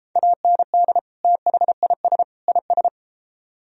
Morse code, heard as a keyed tone, about 32 wpm, 700 Hz. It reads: AND THIS IS